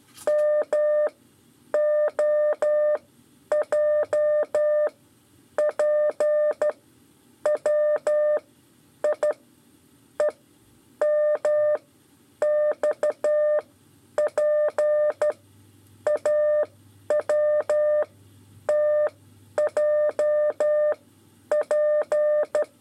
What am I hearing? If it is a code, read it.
Morse code, heard as a keyed tone, about 12 wpm, spaced as for 9 wpm, 600 Hz. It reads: MOJPWI EMXPAWTJP
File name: 5 LEZIONE RIPASSO 2